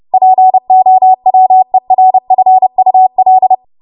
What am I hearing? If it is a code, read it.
Morse code, heard as a keyed tone, about 30 wpm, 750 Hz. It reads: POWERFUL
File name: Morse / 30WPM